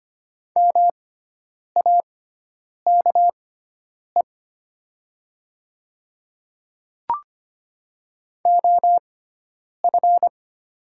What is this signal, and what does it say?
Morse code, heard as a keyed tone, about 25 wpm, spaced as for 9 wpm, 700 Hz. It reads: MAKE OF
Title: 6x Character Spacing - Top 100 Words - 25wpm